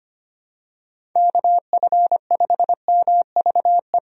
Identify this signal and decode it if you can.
Morse code, heard as a keyed tone, about 25 wpm, 700 Hz. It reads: KF5MVE
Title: USA Call Signs - 6 Characters - Repeated 3x - 25wpm